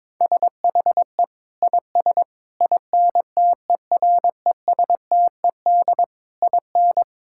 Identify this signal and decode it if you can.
Morse code, heard as a keyed tone, about 22 wpm, 700 Hz. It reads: SHE IS INTERESTED IN
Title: Sentences from Top 1000 Words - Review Part 6 - 22wpm